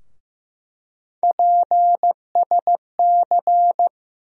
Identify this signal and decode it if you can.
Morse code, heard as a keyed tone, about 15 wpm, 700 Hz. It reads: PSC
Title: USA Call Sign Postfixes - 3 Letters 15wpm